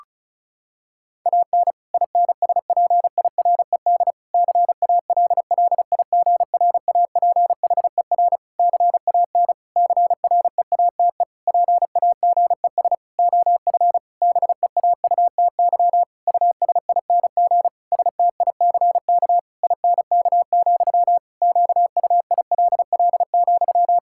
Morse code, heard as a keyed tone, about 36 wpm, 700 Hz. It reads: AN INSPIRED CALLIGRAPHER CAN CREATE PAGES OF BEAUTY USING STICK INK, QUILL,
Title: Pangram Challenge v2 - 35wpm